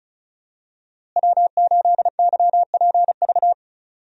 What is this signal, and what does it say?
Morse code, heard as a keyed tone, about 34 wpm, 700 Hz. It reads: W8YPV
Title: USA Call Signs - 5 Characters - Repeated 2x - 35wpm